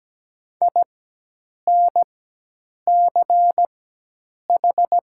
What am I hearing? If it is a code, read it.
Morse code, heard as a keyed tone, about 17 wpm, spaced as for 8 wpm, 700 Hz. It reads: INCH
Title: 4x Character Spacing - Top 500 Words - 17wpm